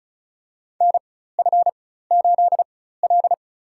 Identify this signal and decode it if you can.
Morse code, heard as a keyed tone, about 34 wpm, spaced as for 16 wpm, 700 Hz. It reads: NF8L
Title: USA Call Signs - 4 Characters - 4x Character Spacing - 35wpm